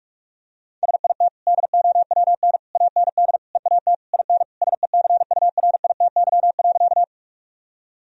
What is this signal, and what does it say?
Morse code, heard as a keyed tone, about 41 wpm, 700 Hz. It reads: SIT DOWN AND EAT IN SECURITY.
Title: Sentences from Top 1000 Words - Review Part 4 - 45wpm